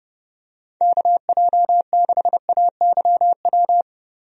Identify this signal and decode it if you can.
Morse code, heard as a keyed tone, about 30 wpm, 700 Hz. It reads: KJ6AYW
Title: USA Call Signs - 6 Characters - Repeated 3x - 30wpm